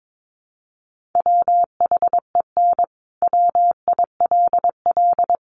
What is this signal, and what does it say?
Morse code, heard as a keyed tone, about 22 wpm, 700 Hz. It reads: WHEN WILL
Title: Sentences from Top 200 Words 22wpm